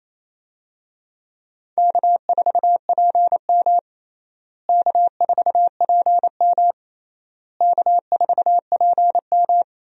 Morse code, heard as a keyed tone, about 28 wpm, 700 Hz. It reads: K4PM K4PM K4PM
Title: USA Call Signs - 4 Characters - Repeated 3x - 28wpm